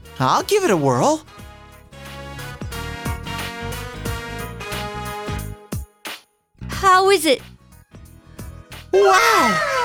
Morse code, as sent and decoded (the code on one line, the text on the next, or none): none